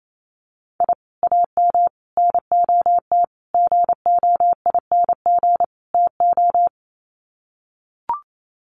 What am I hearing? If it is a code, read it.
Morse code, heard as a keyed tone, about 28 wpm, 700 Hz. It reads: I AM NOT GOING TO
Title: Sets of 5 Words Encore 28wpm